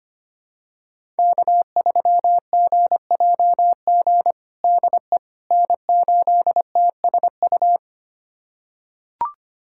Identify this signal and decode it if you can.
Morse code, heard as a keyed tone, about 25 wpm, 700 Hz. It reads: K3GJG DE N8TSU
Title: USA Callsign DE Callsign - 5 Characters - 25wpm